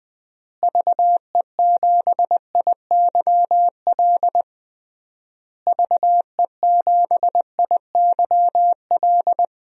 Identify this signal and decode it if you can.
Morse code, heard as a keyed tone, about 21 wpm, 700 Hz. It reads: VE7IYL VE7IYL
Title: DXCC-100 Call Signs - 6 Characters - Repeated 3x - 20wpm